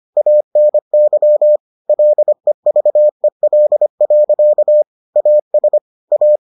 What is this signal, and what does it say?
Morse code, heard as a keyed tone, about 25 wpm, 600 Hz. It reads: ANY LEVEL. AS A